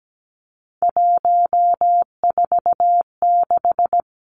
Morse code, heard as a keyed tone, about 17 wpm, 700 Hz. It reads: 146
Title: Three Challenging Letter-Number Combinations 17wpm